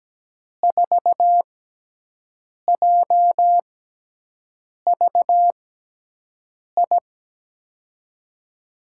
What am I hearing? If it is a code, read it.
Morse code, heard as a keyed tone, about 17 wpm, spaced as for 6 wpm, 700 Hz. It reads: 4JVI
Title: USA Call Signs - 6 Characters - 6x Character Spacing - 17wpm